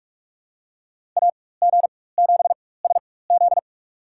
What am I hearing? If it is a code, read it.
Morse code, heard as a keyed tone, about 46 wpm, spaced as for 21 wpm, 700 Hz. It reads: AG7SZ
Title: USA Call Signs - 4x Character Spacing - 45wpm